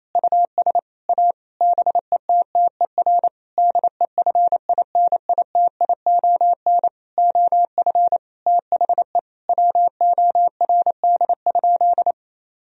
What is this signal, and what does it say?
Morse code, heard as a keyed tone, about 28 wpm, 700 Hz. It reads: US A BETTER DEFINITION OF THE WORD?
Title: Sentences from Top 1000 Words - Review Part 4 - 28wpm